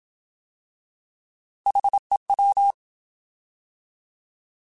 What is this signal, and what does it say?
Morse code, heard as a keyed tone, about 27 wpm, 800 Hz. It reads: HEW
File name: Alphabetical / Alphabetical mp3s / Hs @25